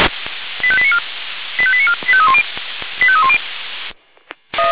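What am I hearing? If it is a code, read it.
Five-tone selective call (ZVEI), heard as quick runs of five tones, tones of about 70 ms. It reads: 85894 85894 85310 85310
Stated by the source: ZVEI five-tone selective call 85310, received via SDR, archived by feuerwehr-orlamuende.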